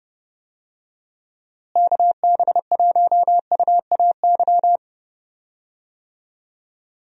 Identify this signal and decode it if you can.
Morse code, heard as a keyed tone, about 30 wpm, 700 Hz. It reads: KB1UAY
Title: USA Call Signs - 6 Characters - Repeated 2x - 30wpm